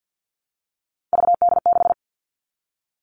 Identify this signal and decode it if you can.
Morse code, heard as a keyed tone, about 50 wpm, 700 Hz. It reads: VD6
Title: All Cut Numbers - 3-Digit 50wpm